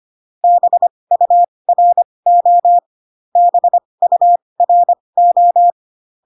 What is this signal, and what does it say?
Morse code, heard as a keyed tone, about 25 wpm, spaced as for 20 wpm, 700 Hz. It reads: BURO BURO